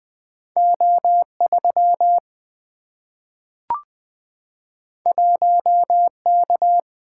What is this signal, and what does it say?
Morse code, heard as a keyed tone, about 20 wpm, 700 Hz. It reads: O 3 1 K
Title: Single-Letter Single-Number 20wpm